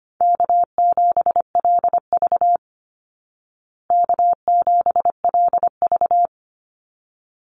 Morse code, heard as a keyed tone, about 25 wpm, 700 Hz. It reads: K7LV K7LV
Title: Easy Call Signs Repeated 2x - 25wpm